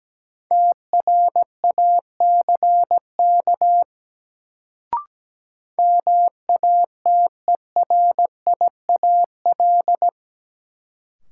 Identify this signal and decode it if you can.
Morse code, heard as a keyed tone, about 17 wpm, 700 Hz. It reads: TRACK MATERIAL
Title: Sentences from Top 1100 Words - New Word Drill Alt - 17wpm